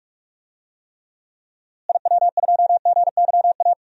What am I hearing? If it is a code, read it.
Morse code, heard as a keyed tone, about 46 wpm, 700 Hz. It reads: IW2GYA
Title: DXCC-100 Call Signs - 6 Characters - Repeated 2x - 45wpm